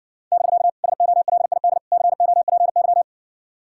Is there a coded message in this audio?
Yes: Morse code from 0.3 to 3.0 s, heard as a keyed tone, about 60 wpm, 700 Hz. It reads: <BK> SOLID COPY